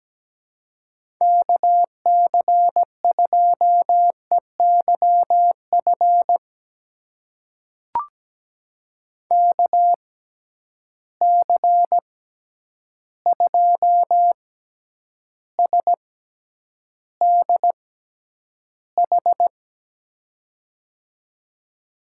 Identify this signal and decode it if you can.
Morse code, heard as a keyed tone, about 17 wpm, spaced as for 9 wpm, 700 Hz. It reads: KC2EYF KC2SDH
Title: USA Call Signs - 6 Characters - 6x Character Spacing - 17wpm